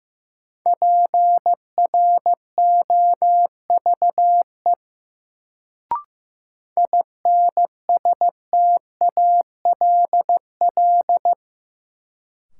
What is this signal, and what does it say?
Morse code, heard as a keyed tone, about 15 wpm, 700 Hz. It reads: PROVE INSTALL
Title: Sentences from Top 1600 Words - New Word Drill - 15wpm